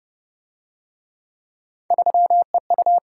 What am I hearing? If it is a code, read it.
Morse code, heard as a keyed tone, about 25 wpm, 700 Hz. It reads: 3EU